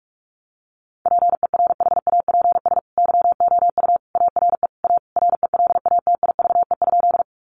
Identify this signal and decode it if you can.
Morse code, heard as a keyed tone, about 44 wpm, 700 Hz. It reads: PERHAPS YOU ARE A RELATIVE?
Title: Sentences from Top 1400 Words 45wpm